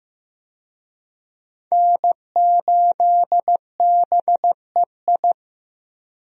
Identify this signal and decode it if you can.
Morse code, heard as a keyed tone, about 16 wpm, 700 Hz. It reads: N8BEI
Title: USA Call Signs - 5 Characters - Repeated 3x - 15wpm